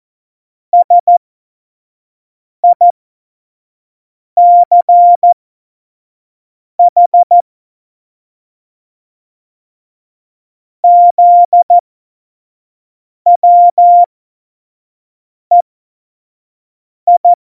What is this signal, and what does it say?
Morse code, heard as a keyed tone, about 14 wpm, spaced as for 5 wpm, 700 Hz. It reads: SICH ZWEI
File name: Morsekurs / Witze / Witz 31.Juli 2018, 25 F70